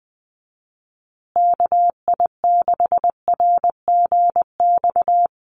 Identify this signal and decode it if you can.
Morse code, heard as a keyed tone, about 20 wpm, 700 Hz. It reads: KI6RGX
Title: USA Call Signs Repeated 2x 20wpm